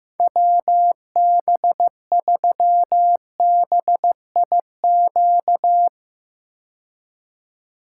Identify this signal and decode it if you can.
Morse code, heard as a keyed tone, about 15 wpm, 700 Hz. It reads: WB3BIQ
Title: K1USN SST QSO Practice - 15wpm